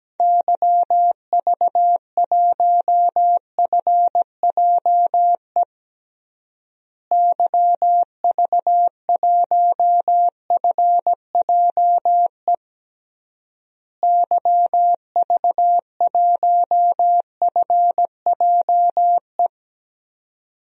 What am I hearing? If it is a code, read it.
Morse code, heard as a keyed tone, about 17 wpm, 700 Hz. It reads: YV1FJE YV1FJE YV1FJE
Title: DXCC-100 Call Signs - 6 Characters - Repeated 3x - 17wpm